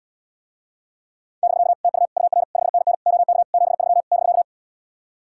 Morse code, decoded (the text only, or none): <BK> TU UR 5NN OK 73 <BK>